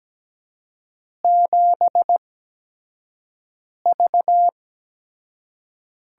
Morse code, heard as a keyed tone, about 17 wpm, spaced as for 5 wpm, 700 Hz. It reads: 7V
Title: USA Call Signs - 6 Characters - 8x Character Spacing - 17wpm